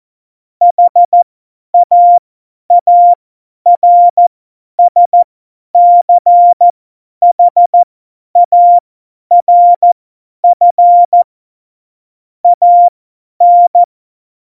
Morse code, heard as a keyed tone, about 14 wpm, spaced as for 10 wpm, 700 Hz. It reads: HAARSCHARF AN
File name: Morsekurs / Witze / Witz 24 April 2018, 50 F70